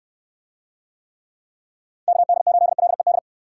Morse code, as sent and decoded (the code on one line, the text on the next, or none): -.- -.. ---.. .-.. -..
KD8LED